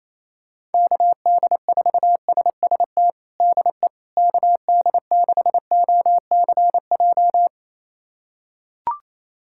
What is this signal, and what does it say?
Morse code, heard as a keyed tone, about 28 wpm, 700 Hz. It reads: KD4SST DE KD6OCJ
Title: USA Callsign DE Callsign - 6 Characters - 28wpm